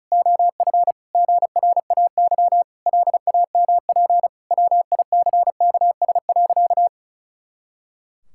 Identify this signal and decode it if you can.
Morse code, heard as a keyed tone, about 34 wpm, 700 Hz. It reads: OF GRAY LAMP WICKS.
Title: Pangram Challenge v2 - 35wpm